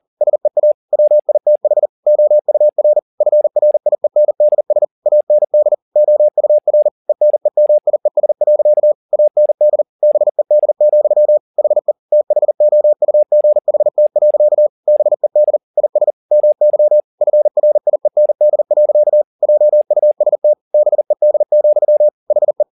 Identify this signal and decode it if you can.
Morse code, heard as a keyed tone, about 40 wpm, 600 Hz. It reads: SEA WITH OUR FRIENDS AND OUR ENEMIES. AND BED, HE THOUGHT. BED IS MY FRIEND. JUST BED, HE
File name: the old man and the sea 2 40wpm